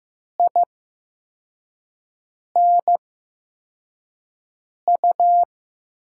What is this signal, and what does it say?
Morse code, heard as a keyed tone, about 15 wpm, spaced as for 4 wpm, 700 Hz. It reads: INU
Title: USA Call Signs - 8x Character Spacing - 15wpm